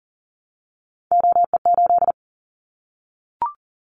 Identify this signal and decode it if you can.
Morse code, heard as a keyed tone, about 40 wpm, 700 Hz. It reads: OE8
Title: Callsign Prefixes for Top 100 DXCC - 40wpm